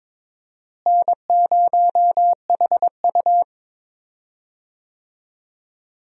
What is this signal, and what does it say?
Morse code, heard as a keyed tone, about 21 wpm, 700 Hz. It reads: N0HU
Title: USA Call Signs - 4 Characters - Repeated 2x - 22wpm